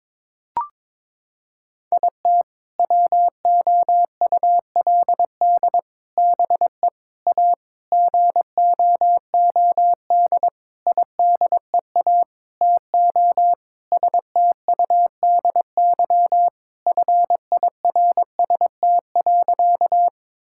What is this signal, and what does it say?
Morse code, heard as a keyed tone, about 22 wpm, 700 Hz. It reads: IT WOULD BE A GOOD IDEA TO STUDY FIRST.
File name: Sentences with 7-Word Phrases - 22wpm